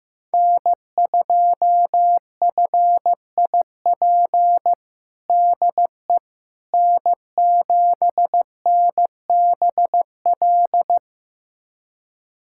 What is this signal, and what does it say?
Morse code, heard as a keyed tone, about 15 wpm, 700 Hz. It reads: N2FIP DE N7NBL
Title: USA Callsign DE Callsign - 5 Characters - 15wpm